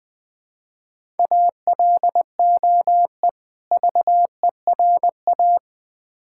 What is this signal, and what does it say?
Morse code, heard as a keyed tone, about 21 wpm, 700 Hz. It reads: ALOE VERA
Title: Difficult Words with Hints - 20wpm